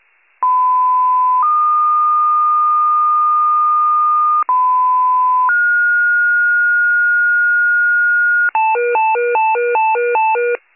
Two-tone page, heard as a one-second tone, then a three-second tone, about 1010 then 1220 Hz, about 1010 then 1430 Hz, with noise. It is heard from 0.4 to 8.5 s.
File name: pager / 94 FIRE / wcalert